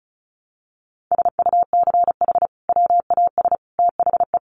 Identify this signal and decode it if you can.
Morse code, heard as a keyed tone, about 34 wpm, 700 Hz. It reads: SUCH WAS THE
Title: Sentences from Top 1200 Words 35wpm